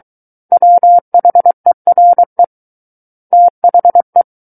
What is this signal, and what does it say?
Morse code, heard as a keyed tone, about 23 wpm, 700 Hz. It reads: WHERE THE
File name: Book One / 14-Chapter 14 - 23 WPM